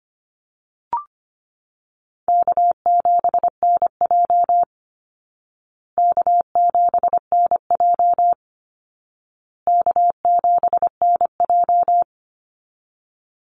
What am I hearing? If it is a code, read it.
Morse code, heard as a keyed tone, about 25 wpm, 700 Hz. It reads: K7NJ K7NJ K7NJ
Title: Easy Call Signs Repeated 3x - 25wpm